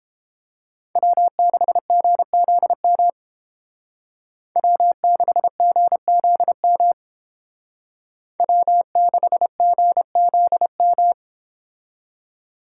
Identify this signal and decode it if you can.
Morse code, heard as a keyed tone, about 28 wpm, 700 Hz. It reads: W6GZM W6GZM W6GZM